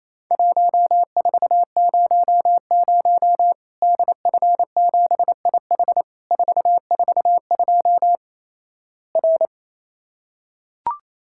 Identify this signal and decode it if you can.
Morse code, heard as a keyed tone, about 27 wpm, 700 Hz. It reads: 1400 DF7IH 442 R
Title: Worked All Europe DX Contest - QTC Exchange - 28wpm